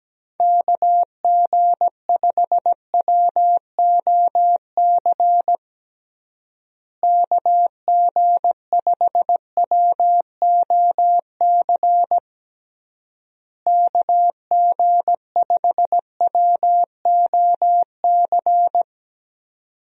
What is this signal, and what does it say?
Morse code, heard as a keyed tone, about 17 wpm, 700 Hz. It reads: KG5WOC KG5WOC KG5WOC
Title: USA Call Signs - 6 Characters - Repeated 3x - 17wpm